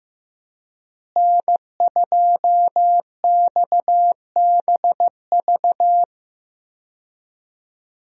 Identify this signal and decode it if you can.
Morse code, heard as a keyed tone, about 15 wpm, 700 Hz. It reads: N2XBV
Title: USA Call Signs - 5 Characters - Repeated 3x - 15wpm